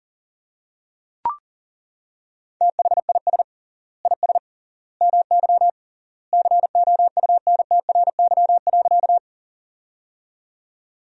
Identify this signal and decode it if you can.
Morse code, heard as a keyed tone, about 40 wpm, 700 Hz. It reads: THIS IS MY COUNTRY.
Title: 3x Word-Spacing - Sentences from Top 300 Words - 40wpm